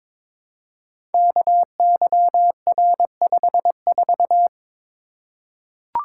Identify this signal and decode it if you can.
Morse code, heard as a keyed tone, about 22 wpm, 700 Hz. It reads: KYR54
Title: Code Groups - 5 Letter:Numbers - 22wpm